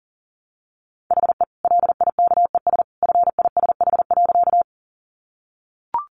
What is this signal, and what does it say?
Morse code, heard as a keyed tone, about 40 wpm, 700 Hz. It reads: HE LIKES FISH.
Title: Easy Phrases 40wpm